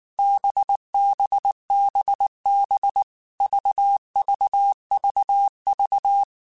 Morse code, heard as a keyed tone, about 19 wpm, 800 Hz. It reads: BBBB VVVV
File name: Basic Letter Groups By7Groups / VB46 / VB46@18